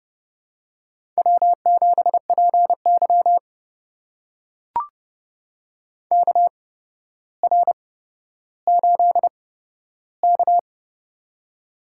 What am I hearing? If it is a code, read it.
Morse code, heard as a keyed tone, about 30 wpm, spaced as for 13 wpm, 700 Hz. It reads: W7PY KR8K
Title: USA Call Signs - 4 Characters - 8x Character Spacing - 30wpm